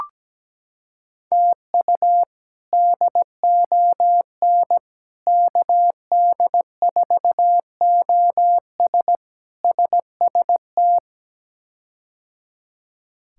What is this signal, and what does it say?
Morse code, heard as a keyed tone, about 17 wpm, 700 Hz. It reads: TU DON KD4OS SST
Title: K1USN SST QSO Practice - 17wpm